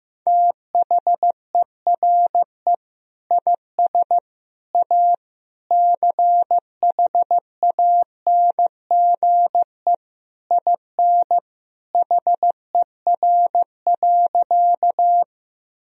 Morse code, heard as a keyed tone, about 16 wpm, 700 Hz. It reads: THERE IS A CHANGE IN HER.
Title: Sentences from Top 1000 Words - Review Part 1 - 15wpm